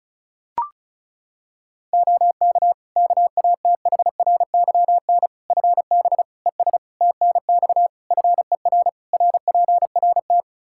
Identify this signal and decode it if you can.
Morse code, heard as a keyed tone, about 34 wpm, 700 Hz. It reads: OK KATHRYN FB ES TNX FER RPRT
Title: K1USN SST QSO Practice 35wpm